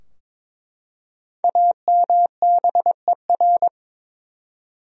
Morse code, heard as a keyed tone, about 21 wpm, 700 Hz. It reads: AMBER
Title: USA Common Names - 22wpm